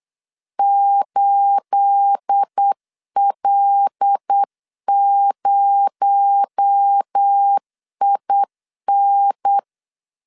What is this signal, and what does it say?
Morse code, heard as a keyed tone, about 8 wpm, 800 Hz. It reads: 8L0IN
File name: 8wpm